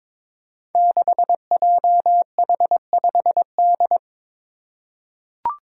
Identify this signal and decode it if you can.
Morse code, heard as a keyed tone, about 23 wpm, 700 Hz. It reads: 6JH5D
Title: Five Challenging Letter-Number Combinations 22wpm